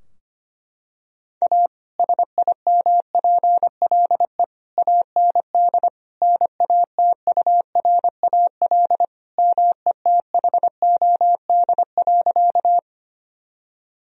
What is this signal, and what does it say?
Morse code, heard as a keyed tone, about 25 wpm, 700 Hz. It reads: A SIMPLE AND NATURAL METHOD.
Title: Sentences from Top 1000 Words - Review Part 4 - 25wpm